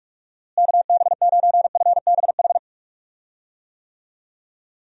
Morse code, heard as a keyed tone, about 43 wpm, 700 Hz. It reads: KB9FBH
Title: USA Call Signs - 6 Characters - Repeated 2x - 45wpm